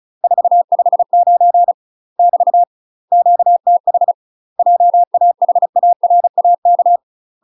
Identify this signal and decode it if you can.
Morse code, heard as a keyed tone, about 34 wpm, 700 Hz. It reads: 459 <BT> QTH JAHARAK
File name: UK MASTER QSOs TRAINING / 35WPM / UK MASTER QSOs 0015